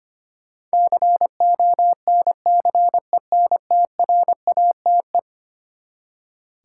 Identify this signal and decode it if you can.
Morse code, heard as a keyed tone, about 25 wpm, 700 Hz. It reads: CONCENTRATE
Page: morse-code-ninja-practice-25wpm